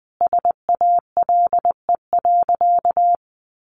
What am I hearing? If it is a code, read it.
Morse code, heard as a keyed tone, about 20 wpm, 700 Hz. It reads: SALE.
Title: Sentences from Top 1400 Words 20wpm